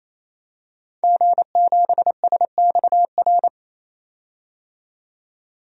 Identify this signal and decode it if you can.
Morse code, heard as a keyed tone, about 28 wpm, 700 Hz. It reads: G7SXR
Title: DXCC-100 Call Signs - 5 Characters - 28wpm